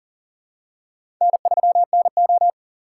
Morse code, heard as a keyed tone, about 42 wpm, 700 Hz. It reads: N3NO
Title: USA Call Signs - 4 Characters - 8x Character Spacing - 40wpm